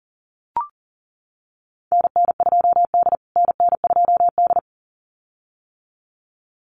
Morse code, heard as a keyed tone, about 40 wpm, 700 Hz. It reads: NN2D NN2D
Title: USA Call Signs Repeated 2x 40wpm